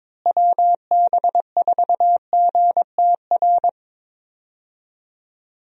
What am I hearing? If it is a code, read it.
Morse code, heard as a keyed tone, about 22 wpm, 700 Hz. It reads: WB4GTR
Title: USA Call Signs - 6 Characters - Repeated 3x - 22wpm